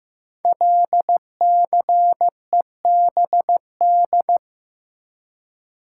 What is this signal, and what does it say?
Morse code, heard as a keyed tone, about 15 wpm, 700 Hz. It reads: LCEBD
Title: Code Groups - 5 Letters - 15wpm